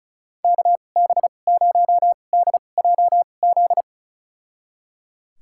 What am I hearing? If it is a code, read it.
Morse code, heard as a keyed tone, about 34 wpm, spaced as for 25 wpm, 700 Hz. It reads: KB0DJZ
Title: USA Call Signs - 2x Character Spacing - 35wpm